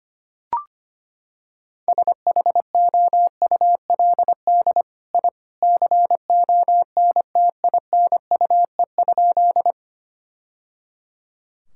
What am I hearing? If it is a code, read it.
Morse code, heard as a keyed tone, about 25 wpm, 700 Hz. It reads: SHOULD I CONTINUE?
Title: Sentences from Top 1000 Words - Review Part 6 - 25wpm